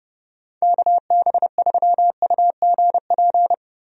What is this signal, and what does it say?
Morse code, heard as a keyed tone, about 31 wpm, 700 Hz. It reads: KB3UGP
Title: USA Call Signs - 6 Characters - 30wpm